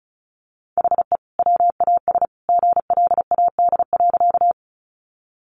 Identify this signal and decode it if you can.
Morse code, heard as a keyed tone, about 34 wpm, 700 Hz. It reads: HE WAS GLAD.
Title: Sentences from Top 800 Words 35wpm